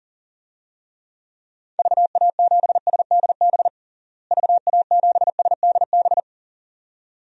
Morse code, heard as a keyed tone, about 40 wpm, 700 Hz. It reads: VA7SDB VA7SDB
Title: DXCC-100 Call Signs - 6 Characters - Repeated 2x - 40wpm